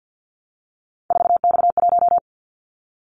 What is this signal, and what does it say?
Morse code, heard as a keyed tone, about 50 wpm, 700 Hz. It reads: 4X1
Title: Callsign Prefixes for Top 50 DXCC - 50wpm